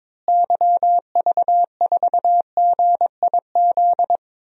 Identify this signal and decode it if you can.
Morse code, heard as a keyed tone, about 22 wpm, 700 Hz. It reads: YV4GIZ
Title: DXCC-100 Call Signs - 6 Characters - Repeated 2x - 22wpm